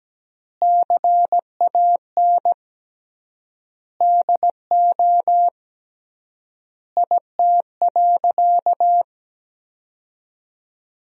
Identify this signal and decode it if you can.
Morse code, heard as a keyed tone, about 17 wpm, 700 Hz. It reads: CAN DO IT.
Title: 3x Word-Spacing - Sentences from Top 100 Words - 17wpm